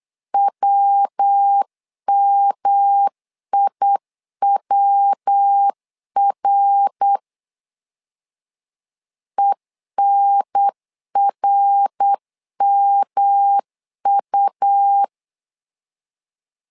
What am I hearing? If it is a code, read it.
Morse code, heard as a keyed tone, about 8 wpm, 800 Hz. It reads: WMIWR ENRMU
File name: LEZIONI / 3 MWRO / 3 MWR 8wpm 2444